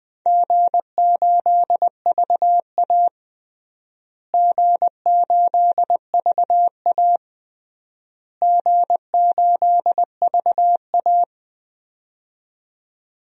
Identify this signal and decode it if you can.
Morse code, heard as a keyed tone, about 20 wpm, 700 Hz. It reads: G8VA G8VA G8VA
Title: DXCC-100 Call Signs - 4 Characters - Repeated 3x - 20wpm